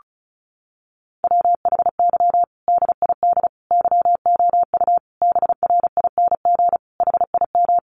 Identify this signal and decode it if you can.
Morse code, heard as a keyed tone, about 34 wpm, 700 Hz. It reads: WHY DID YOU BRING HIM
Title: Sentences from Top 500 Words 35wpm